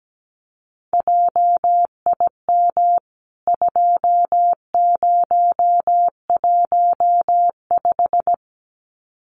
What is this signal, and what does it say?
Morse code, heard as a keyed tone, about 17 wpm, 700 Hz. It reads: JIM 2015
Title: CWT Exchange 17wpm